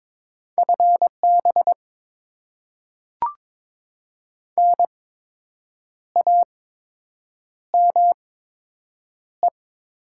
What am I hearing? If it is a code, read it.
Morse code, heard as a keyed tone, about 22 wpm, spaced as for 7 wpm, 700 Hz. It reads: FB NAME
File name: Top 20 QSO Elements - 8x Character Spacing - 22wpm